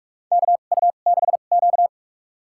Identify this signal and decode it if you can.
Morse code, heard as a keyed tone, about 44 wpm, spaced as for 32 wpm, 700 Hz. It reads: KU6Q
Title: USA Call Signs - 4 Characters - 2x Character Spacing - 45wpm